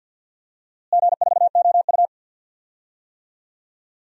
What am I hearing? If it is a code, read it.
Morse code, heard as a keyed tone, about 46 wpm, 700 Hz. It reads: G4OU
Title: DXCC-100 Call Signs - 4 Characters - Repeated 2x - 50wpm